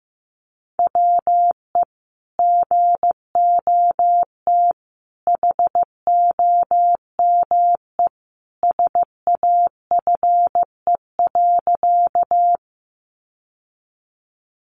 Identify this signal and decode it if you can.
Morse code, heard as a keyed tone, about 15 wpm, 700 Hz. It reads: WE GOT HOME SAFE.
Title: Sentences from Top 900 Words 15wpm